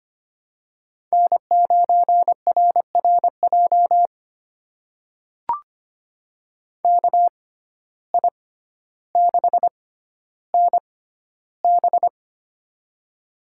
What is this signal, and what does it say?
Morse code, heard as a keyed tone, about 25 wpm, spaced as for 13 wpm, 700 Hz. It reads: N9RRJ KI6NB
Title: USA Call Signs - 5 Characters - 6x Character Spacing - 25wpm